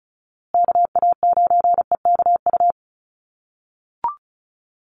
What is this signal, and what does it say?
Morse code, heard as a keyed tone, about 34 wpm, 700 Hz. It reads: KA9EKU
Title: USA Call Signs Repeated 2x 35wpm